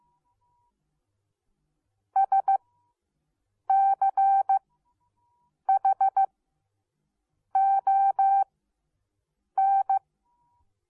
Morse code, heard as a keyed tone, about 15 wpm, spaced as for 6 wpm, 780 Hz. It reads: SCHON